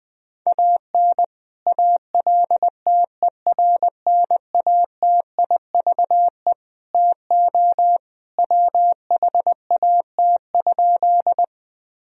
Morse code, heard as a keyed tone, about 20 wpm, 700 Hz. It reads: AN ALTERNATIVE TO WHAT?
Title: Sentences from Top 1000 Words - Review Part 4 - 20wpm